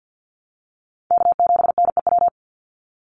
Morse code, heard as a keyed tone, about 49 wpm, 700 Hz. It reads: K7NEW
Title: NA SOTA Chaser Callsigns 50wpm